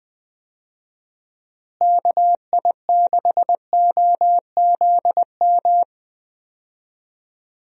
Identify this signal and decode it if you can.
Morse code, heard as a keyed tone, about 20 wpm, 700 Hz. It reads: KI6OZM
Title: USA Call Signs - 6 Characters - Repeated 3x - 20wpm